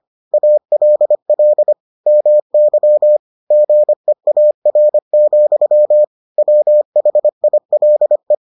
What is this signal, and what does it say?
Morse code, heard as a keyed tone, about 25 wpm, 600 Hz. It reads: ALL MY GEAR, WHILE